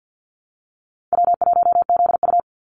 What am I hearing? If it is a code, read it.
Morse code, heard as a keyed tone, about 48 wpm, 700 Hz. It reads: W1ZU
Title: NA SOTA Activator Callsigns 50wpm